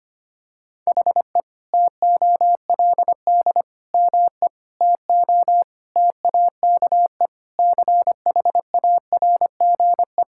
Morse code, heard as a keyed tone, about 25 wpm, 700 Hz. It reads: HE TOLD ME TO TAKE CHARGE
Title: Sentences from Top 1000 Words - Review Part 6 - 25wpm